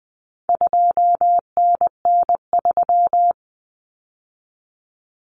Morse code, heard as a keyed tone, about 20 wpm, 700 Hz. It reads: 2NN3
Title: Common Cut Numbers - 4-Digit 20wpm